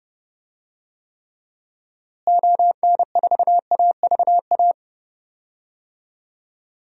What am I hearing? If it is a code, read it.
Morse code, heard as a keyed tone, about 30 wpm, 700 Hz. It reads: ON4AVA
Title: DXCC-100 Call Signs - 6 Characters - Repeated 3x - 30wpm